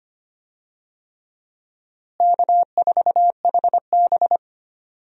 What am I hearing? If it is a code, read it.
Morse code, heard as a keyed tone, about 25 wpm, 700 Hz. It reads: K4HB
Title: USA Call Signs - 4 Characters - Repeated 3x - 25wpm